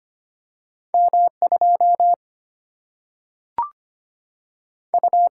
Morse code, heard as a keyed tone, about 25 wpm, 700 Hz. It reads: M2 U
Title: Single-Letter Single-Number 25wpm